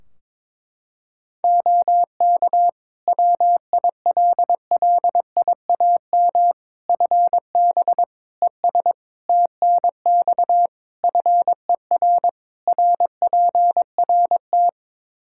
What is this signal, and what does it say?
Morse code, heard as a keyed tone, about 22 wpm, 700 Hz. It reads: OK WILLIAM FB ES TNX FER RPRT
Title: Practice QSO - Exchange 2 - Info - 22wpm